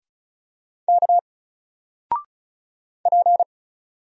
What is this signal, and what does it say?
Morse code, heard as a keyed tone, about 37 wpm, 700 Hz. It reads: K P
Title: Single Letter-Number 35wpm